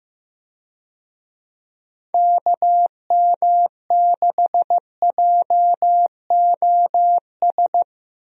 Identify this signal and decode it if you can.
Morse code, heard as a keyed tone, about 15 wpm, 700 Hz. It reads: KM6JOS
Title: USA Call Signs - 6 Characters - Repeated 3x - 15wpm